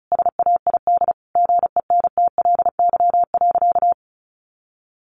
Morse code, heard as a keyed tone, about 34 wpm, 700 Hz. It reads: SAID GENTLY.
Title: Sentences from Top 1500 Words 35wpm